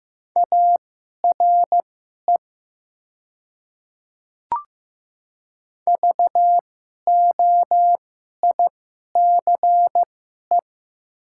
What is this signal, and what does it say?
Morse code, heard as a keyed tone, about 15 wpm, spaced as for 11 wpm, 700 Hz. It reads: ARE VOICE
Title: 2x Character Spacing - Top 500 Words - 15wpm